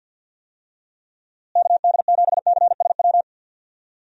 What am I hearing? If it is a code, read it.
Morse code, heard as a keyed tone, about 53 wpm, 700 Hz. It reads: KD7CSW
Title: USA Call Signs - 6 Characters - Repeated 3x - 50wpm